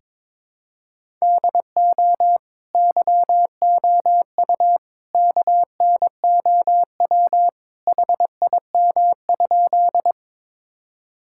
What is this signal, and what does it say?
Morse code, heard as a keyed tone, about 22 wpm, 700 Hz. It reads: DO YOU KNOW HIM?
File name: practice / 22wpm / Sentences with 2-Word Phrases - 22wpm